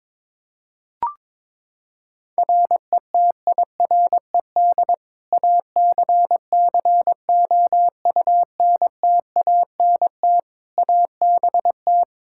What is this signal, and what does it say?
Morse code, heard as a keyed tone, about 22 wpm, 700 Hz. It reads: RETIRED ACCOUNTANT ABT
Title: K1USN SST QSO Practice 22wpm